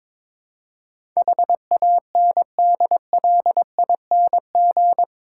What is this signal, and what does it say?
Morse code, heard as a keyed tone, about 22 wpm, 700 Hz. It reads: HANDLING
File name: practice / 22wpm / Eight Letter Words - 22wpm